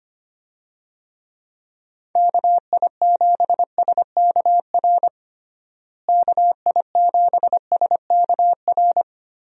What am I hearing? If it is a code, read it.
Morse code, heard as a keyed tone, about 25 wpm, 700 Hz. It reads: KI7SKR KI7SKR
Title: USA Call Signs - 6 Characters - Repeated 2x - 25wpm